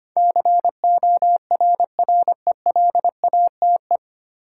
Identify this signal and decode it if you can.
Morse code, heard as a keyed tone, about 25 wpm, 700 Hz. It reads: CORRELATE
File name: Nine Letter Words - 25wpm